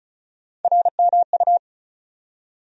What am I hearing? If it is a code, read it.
Morse code, heard as a keyed tone, about 34 wpm, 700 Hz. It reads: RMU
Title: USA Call Sign Postfixes - 3 Letters 35wpm